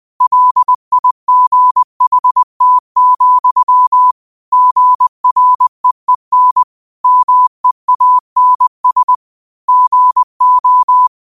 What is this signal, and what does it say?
Morse code, heard as a keyed tone, about 20 wpm, 1000 Hz. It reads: LIGHT, GREEN MEANS GO
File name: qotdfiles / 20wpm